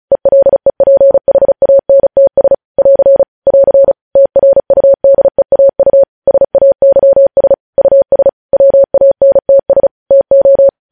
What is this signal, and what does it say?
Morse code, heard as a keyed tone, about 36 wpm, 550 Hz. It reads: ELEPHANTS <AR> <AR> TRUDEAU SAYS US WANTS TO
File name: podcast / News headlines-1x-35x35